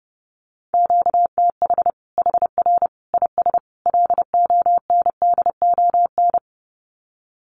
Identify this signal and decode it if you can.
Morse code, heard as a keyed tone, about 30 wpm, 700 Hz. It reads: QTH HR IS LONDON
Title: Common QSO Phrases 30wpm